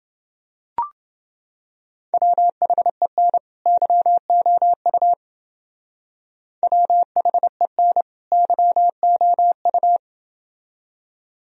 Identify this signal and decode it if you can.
Morse code, heard as a keyed tone, about 29 wpm, 700 Hz. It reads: WHEN YOU WHEN YOU